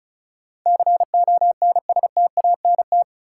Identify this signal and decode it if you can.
Morse code, heard as a keyed tone, about 36 wpm, 700 Hz. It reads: CONSTANT
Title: Sentences from Top 1500 Words - New Word Drill - 35wpm